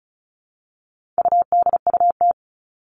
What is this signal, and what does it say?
Morse code, heard as a keyed tone, about 34 wpm, 700 Hz. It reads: UDUT
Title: All Cut Numbers - 4-Digit 35wpm